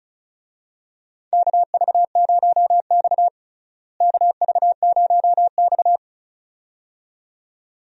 Morse code, heard as a keyed tone, about 34 wpm, 700 Hz. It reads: KV0X KV0X
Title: USA Call Signs - 4 Characters - Repeated 2x - 35wpm